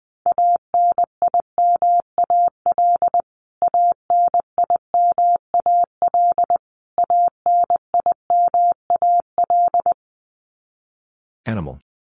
Morse code, heard as a keyed tone, about 20 wpm, 700 Hz. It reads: ANIMAL ANIMAL ANIMAL
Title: Top 500 Words Repeated 20wpm